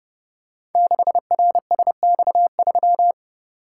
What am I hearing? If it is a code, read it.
Morse code, heard as a keyed tone, about 30 wpm, 700 Hz. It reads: 6RSX3
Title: Code Groups - 5 Letter:Numbers - 30wpm